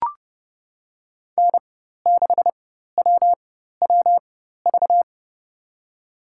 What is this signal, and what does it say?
Morse code, heard as a keyed tone, about 30 wpm, spaced as for 14 wpm, 700 Hz. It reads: N6WWV
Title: USA Call Signs - 5 Characters - 4x Character Spacing - 30wpm